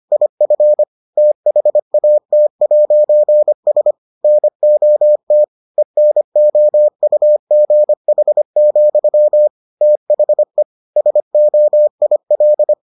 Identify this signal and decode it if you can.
Morse code, heard as a keyed tone, about 25 wpm, 600 Hz. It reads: IF THAT'S NOT ENOUGH, THE SOIL